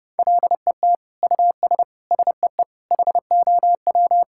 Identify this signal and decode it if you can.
Morse code, heard as a keyed tone, about 30 wpm, 700 Hz. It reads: LET US SEE HOW
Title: Sentences from Top 1000 Words - Review Part 6 - 30wpm